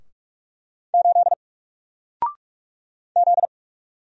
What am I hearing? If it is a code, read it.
Morse code, heard as a keyed tone, about 44 wpm, 700 Hz. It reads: 8 Z
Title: Single Letter-Number 45wpm